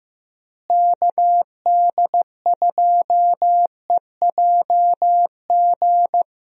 Morse code, heard as a keyed tone, about 15 wpm, 700 Hz. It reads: KD2EJG
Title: USA Call Signs - 6 Characters - Repeated 3x - 15wpm